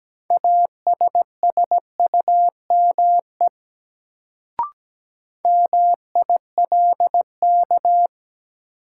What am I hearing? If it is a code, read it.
Morse code, heard as a keyed tone, about 17 wpm, 700 Hz. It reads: ASSUME MILK
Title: Sentences from Top 1500 Words - New Word Drill Alt - 17wpm